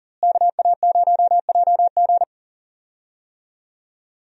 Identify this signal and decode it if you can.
Morse code, heard as a keyed tone, about 40 wpm, 700 Hz. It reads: KA0JG
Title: USA Call Signs - 5 Characters - Repeated 3x - 40wpm